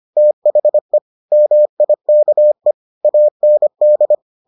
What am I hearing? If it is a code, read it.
Morse code, heard as a keyed tone, about 25 wpm, 600 Hz. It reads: THE MIKE AND